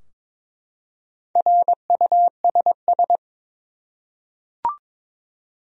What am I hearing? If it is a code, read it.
Morse code, heard as a keyed tone, about 22 wpm, 700 Hz. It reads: RUSS
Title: USA Common Names - Max 4 Letters - 22wpm